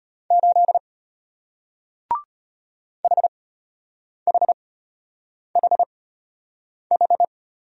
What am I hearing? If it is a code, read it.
Morse code, heard as a keyed tone, about 33 wpm, spaced as for 8 wpm, 690 Hz. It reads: 8 HHHH